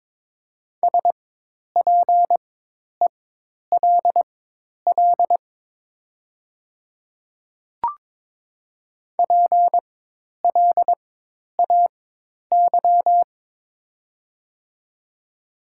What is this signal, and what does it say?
Morse code, heard as a keyed tone, about 22 wpm, spaced as for 10 wpm, 700 Hz. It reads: SPELL PLAY
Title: 4x Character Spacing - Top 300 Words - 22wpm